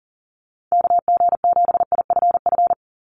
Morse code, heard as a keyed tone, about 40 wpm, 700 Hz. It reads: KG7IFF
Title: USA Call Signs 40wpm